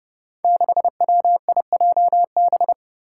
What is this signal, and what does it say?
Morse code, heard as a keyed tone, about 30 wpm, 700 Hz. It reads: 6WIJB
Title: Five Challenging Letter-Number Combinations 30wpm